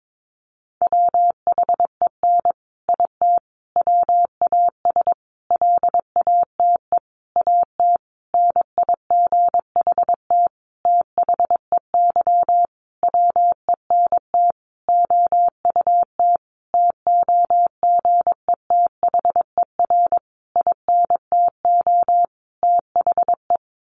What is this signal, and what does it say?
Morse code, heard as a keyed tone, about 22 wpm, 700 Hz. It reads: WHEN IT WAS LATE AT NIGHT THEY WENT OUT TOGETHER INTO THE
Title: Sentences from Top 600 Words 22wpm